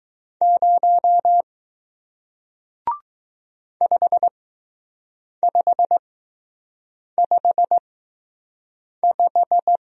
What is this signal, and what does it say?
Morse code, heard as a keyed tone, about 20 wpm, spaced as for 6 wpm, 700 Hz. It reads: 0 5555